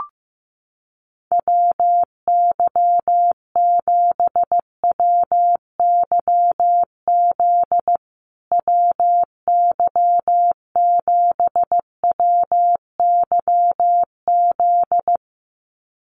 Morse code, heard as a keyed tone, about 15 wpm, 700 Hz. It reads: WY7WYZ WY7WYZ
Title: USA Call Signs Repeated 2x 15wpm